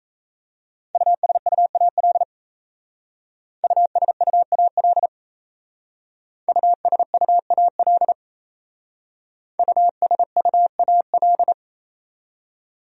Morse code, heard as a keyed tone, about 35 wpm, 700 Hz. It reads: USUAL USUAL USUAL USUAL